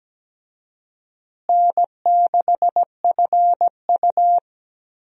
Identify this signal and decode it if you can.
Morse code, heard as a keyed tone, about 17 wpm, 700 Hz. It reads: N6FU
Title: USA Call Signs - 4 Characters - Repeated 3x - 17wpm